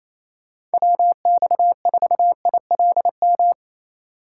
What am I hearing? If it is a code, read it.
Morse code, heard as a keyed tone, about 34 wpm, 700 Hz. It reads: WX4ILM
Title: Speed Racing - USA Call Signs 28wpm